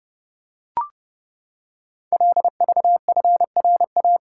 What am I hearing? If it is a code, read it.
Morse code, heard as a keyed tone, about 30 wpm, 700 Hz. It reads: LVFRA
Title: Code Groups - 5 Letters - 30wpm